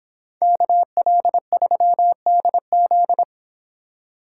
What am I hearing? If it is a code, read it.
Morse code, heard as a keyed tone, about 26 wpm, 700 Hz. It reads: KL3DZ